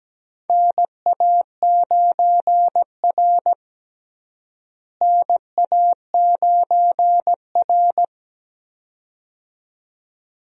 Morse code, heard as a keyed tone, about 17 wpm, 700 Hz. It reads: NA9R NA9R
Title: USA Call Signs - 4 Characters - Repeated 2x - 17wpm